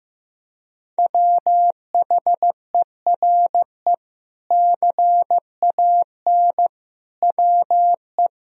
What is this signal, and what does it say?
Morse code, heard as a keyed tone, about 15 wpm, 700 Hz. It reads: WHERE CAN WE
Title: Sentences from Top 1000 Words - Review Part 4 - 15wpm